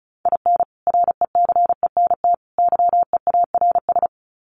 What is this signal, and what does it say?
Morse code, heard as a keyed tone, about 34 wpm, 700 Hz. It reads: IN RECENT YEARS
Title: Sets of 3 Words 35wpm